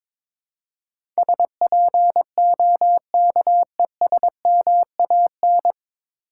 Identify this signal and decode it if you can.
Morse code, heard as a keyed tone, about 22 wpm, 700 Hz. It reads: SPOKESMAN
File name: 9 Letter Words - First 3 Letters Given - 22wpm